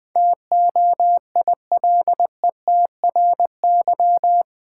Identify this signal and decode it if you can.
Morse code, heard as a keyed tone, about 20 wpm, 700 Hz. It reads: TOILETRY
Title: Difficult Words - Speed-Racing with Hints - 20wpm